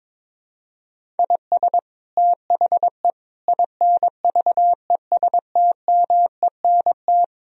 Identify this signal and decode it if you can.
Morse code, heard as a keyed tone, about 22 wpm, 700 Hz. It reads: IS THE INVESTMENT
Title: Sentences from Top 1000 Words - Review Part 4 - 22wpm